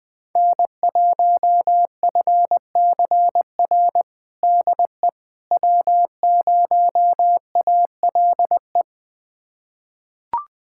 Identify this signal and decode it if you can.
Morse code, heard as a keyed tone, about 20 wpm, 700 Hz. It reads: N1FCR DE W0ALE
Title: USA Callsign DE Callsign - 5 Characters - 20wpm